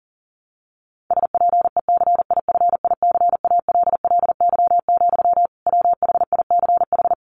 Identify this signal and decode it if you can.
Morse code, heard as a keyed tone, about 40 wpm, 700 Hz. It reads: SPECIFICALLY, WHICH
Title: Sentences from Top 1100 Words 40wpm